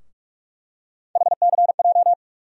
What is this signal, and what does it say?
Morse code, heard as a keyed tone, about 46 wpm, 700 Hz. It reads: HCJ
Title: USA Call Sign Postfixes - 3 Letters 45wpm